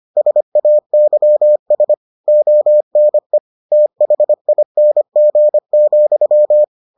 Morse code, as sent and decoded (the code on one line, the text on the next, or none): ... .- -.-- ... / --- -. / - .... .. -. --. --..--
SAYS ONE THING,